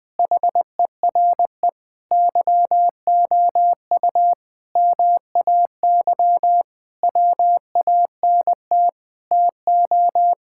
Morse code, heard as a keyed tone, about 21 wpm, 700 Hz. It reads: HERE YOU MAY WANT TO
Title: Sentences with 8-Word Phrases - Spoken Beginning - 20wpm